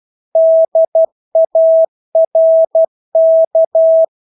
Morse code, heard as a keyed tone, about 12 wpm, 650 Hz. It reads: DARK
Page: Engelske ord 12 wpm